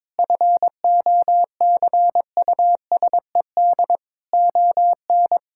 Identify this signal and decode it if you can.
Morse code, heard as a keyed tone, about 22 wpm, 700 Hz. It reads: FOCUSED ON
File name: Sentences with 7-Word Phrases - 22wpm